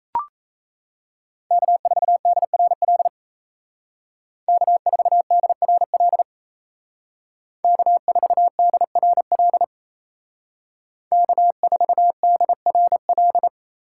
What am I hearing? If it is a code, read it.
Morse code, heard as a keyed tone, about 33 wpm, 700 Hz. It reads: K4DRL K4DRL K4DRL K4DRL
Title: Speed Racing - USA 1x3 Call Signs 28wpm